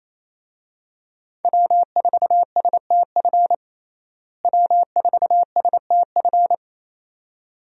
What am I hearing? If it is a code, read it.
Morse code, heard as a keyed tone, about 28 wpm, 700 Hz. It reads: W4STF W4STF
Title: USA Call Signs - 5 Characters - Repeated 2x - 28wpm